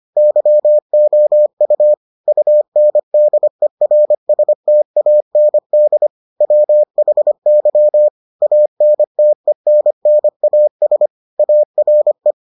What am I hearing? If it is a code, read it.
Morse code, heard as a keyed tone, about 25 wpm, 600 Hz. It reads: YOU UNDERSTAND WHY ANTENNAS ARE